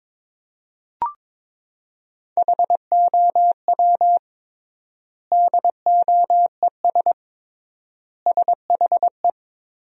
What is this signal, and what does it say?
Morse code, heard as a keyed tone, about 23 wpm, 700 Hz. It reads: HOW DOES SHE
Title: 3x Word-Spacing - Sentences from Top 200 Words - 22wpm